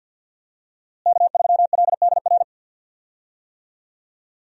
Morse code, heard as a keyed tone, about 51 wpm, 700 Hz. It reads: K3LDR
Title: USA Call Signs - 5 Characters - Repeated 3x - 50wpm